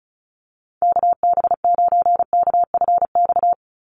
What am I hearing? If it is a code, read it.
Morse code, heard as a keyed tone, about 34 wpm, 700 Hz. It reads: KB9KFX
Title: USA Call Signs 35wpm